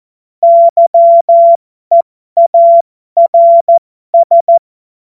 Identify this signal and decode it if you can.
Morse code, heard as a keyed tone, about 14 wpm, spaced as for 12 wpm, 680 Hz. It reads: YEARS
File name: Morsekurs / Witze